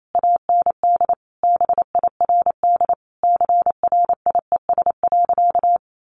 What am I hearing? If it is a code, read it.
Morse code, heard as a keyed tone, about 28 wpm, 700 Hz. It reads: AND BIRD CRIES.
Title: A Sound Of Thunder 28wpm